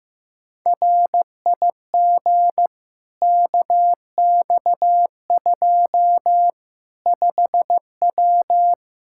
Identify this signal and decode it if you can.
Morse code, heard as a keyed tone, about 15 wpm, 700 Hz. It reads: RIG KX2 5W
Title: Practice QSO - Exchange 2 - Info - 15wpm